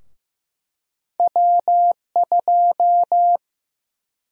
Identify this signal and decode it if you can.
Morse code, heard as a keyed tone, about 15 wpm, 700 Hz. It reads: W2
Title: USA Call Sign Prefixes - 15wpm